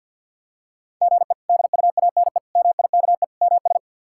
Morse code, heard as a keyed tone, about 51 wpm, 700 Hz. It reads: GE DUANE MIKE MS
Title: K1USN SST QSO Practice - 50wpm